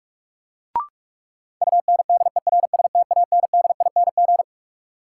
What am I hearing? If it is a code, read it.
Morse code, heard as a keyed tone, about 41 wpm, 700 Hz. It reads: UNDERSTANDING
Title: Sentences from Top 1400 Words - New Word Drill Alt - 45wpm